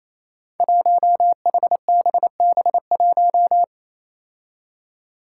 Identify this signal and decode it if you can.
Morse code, heard as a keyed tone, about 27 wpm, 700 Hz. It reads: 1HBB1
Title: Five Challenging Letter-Number Combinations 28wpm